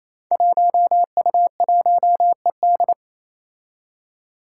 Morse code, heard as a keyed tone, about 28 wpm, 700 Hz. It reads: 1U1ED
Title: Five Challenging Letter-Number Combinations 28wpm